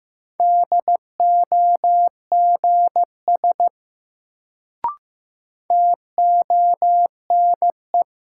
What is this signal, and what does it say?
Morse code, heard as a keyed tone, about 15 wpm, 700 Hz. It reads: DOGS TONE
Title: Sentences from Top 1600 Words - New Word Drill Alt - 15wpm